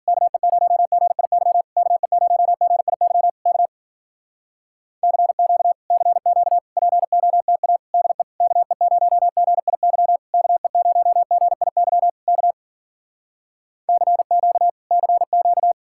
Morse code, heard as a keyed tone, about 53 wpm, 700 Hz. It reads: KE0GIY KE0GIY K CQ CQ POTA DE KE0GIY KE0GIY K CQ CQ